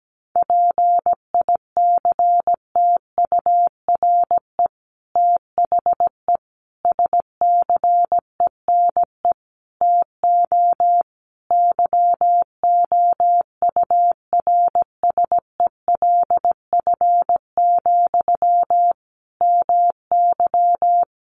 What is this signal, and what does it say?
Morse code, heard as a keyed tone, about 17 wpm, 700 Hz. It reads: PICTURE THE SCENE TO YOURSELF, MY
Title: Sentences from Top 1100 Words 17wpm